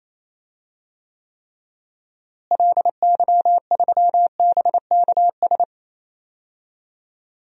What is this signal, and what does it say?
Morse code, heard as a keyed tone, about 27 wpm, 700 Hz. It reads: LY3BKS
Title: DXCC-100 Call Signs - 6 Characters - Repeated 3x - 28wpm